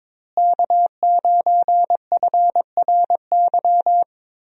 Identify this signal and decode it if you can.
Morse code, heard as a keyed tone, about 22 wpm, 700 Hz. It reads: K9FRY
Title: USA Call Signs - 5 Characters - Repeated 3x - 22wpm